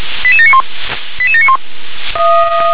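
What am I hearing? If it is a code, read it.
Five-tone selective call (ZVEI), heard as quick runs of five tones, tones of about 70 ms. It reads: 88791 88791
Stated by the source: ZVEI five-tone selective call 88694 received with an SDR.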